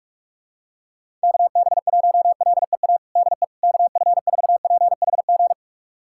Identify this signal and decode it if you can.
Morse code, heard as a keyed tone, about 41 wpm, 700 Hz. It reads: KB1LEA DE KF4PHG
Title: USA Callsign DE Callsign 45wpm